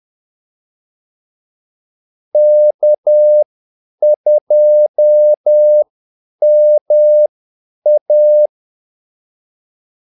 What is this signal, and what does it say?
Morse code, heard as a keyed tone, about 10 wpm, spaced as for 8 wpm, 600 Hz. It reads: K2MA
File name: Callsigns-10-8wpm